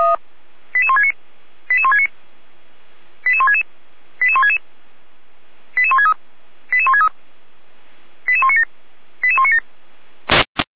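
Five-tone selective call (ZVEI), heard as quick runs of five tones, tones of about 70 ms. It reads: 88169 88169 88166 88166 88163 88163 88197 88197
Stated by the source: ZVEI five-tone selective call received on an SDR.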